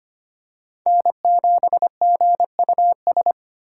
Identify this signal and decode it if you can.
Morse code, heard as a keyed tone, about 25 wpm, 700 Hz. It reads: N7GUS
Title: USA Call Signs - 2x Character Spacing - 25wpm